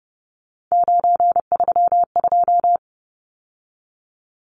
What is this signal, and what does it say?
Morse code, heard as a keyed tone, about 30 wpm, 700 Hz. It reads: 932
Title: Numbers - 3-Digit 30wpm